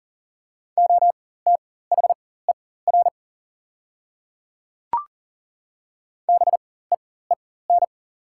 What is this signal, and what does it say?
Morse code, heard as a keyed tone, about 40 wpm, spaced as for 19 wpm, 700 Hz. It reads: OTHER BEEN
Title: 4x Character Spacing - Top 100 Words - 40wpm